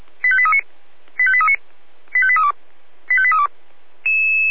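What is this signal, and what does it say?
Five-tone selective call (ZVEI), heard as quick runs of five tones, tones of about 70 ms. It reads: 85839 85839 85832 85832
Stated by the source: ZVEI five-tone call 85830 received by SDR.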